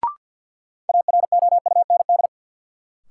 Morse code, heard as a keyed tone, about 53 wpm, 700 Hz. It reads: AROUND
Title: Sentences from Top 600 Words - New Word Drill - 50wpm